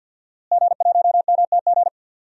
Morse code, heard as a keyed tone, about 46 wpm, 700 Hz. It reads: G1MTG
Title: DXCC-100 Call Signs - 5 Characters - Repeated 2x - 50wpm